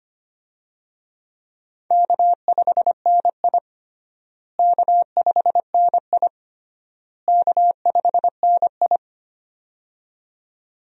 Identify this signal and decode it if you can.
Morse code, heard as a keyed tone, about 25 wpm, 700 Hz. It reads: K5NI K5NI K5NI
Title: USA Call Signs - 4 Characters - Repeated 3x - 25wpm